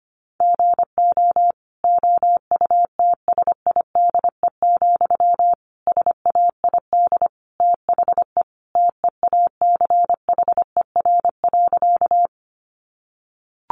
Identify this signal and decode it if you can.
Morse code, heard as a keyed tone, about 25 wpm, 700 Hz. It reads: GO OUTSIDE, SAID THE TEACHER.
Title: Sentences from Top 900 Words 25wpm